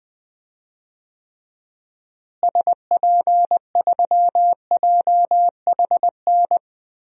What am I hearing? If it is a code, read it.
Morse code, heard as a keyed tone, about 20 wpm, 700 Hz. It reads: SP3JHN
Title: DXCC-100 Call Signs - 6 Characters - Repeated 2x - 20wpm